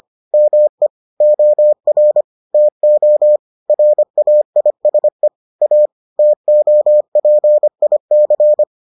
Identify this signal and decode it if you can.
Morse code, heard as a keyed tone, about 25 wpm, 600 Hz. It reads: ME OR TO RAISE A TOPIC